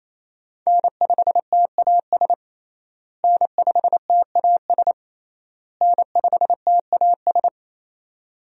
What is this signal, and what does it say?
Morse code, heard as a keyed tone, about 28 wpm, 700 Hz. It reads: N5TAS N5TAS N5TAS
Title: USA Call Signs - 5 Characters - Repeated 3x - 28wpm